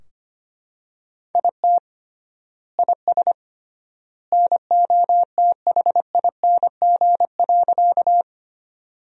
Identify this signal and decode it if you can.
Morse code, heard as a keyed tone, about 25 wpm, 700 Hz. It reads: IT IS NOTHING.
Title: 3x Word-Spacing - Sentences from Top 500 Words - 25wpm